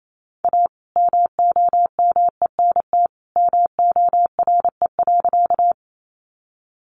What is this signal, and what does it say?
Morse code, heard as a keyed tone, about 28 wpm, 700 Hz. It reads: A MOMENT MORE.
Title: Sentences from Top 1000 Words 28wpm